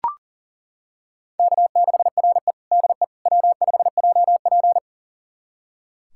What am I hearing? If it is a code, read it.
Morse code, heard as a keyed tone, about 40 wpm, 700 Hz. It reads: K6RE DE W5JP
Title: USA Callsign DE Callsign - 4 Characters - 40wpm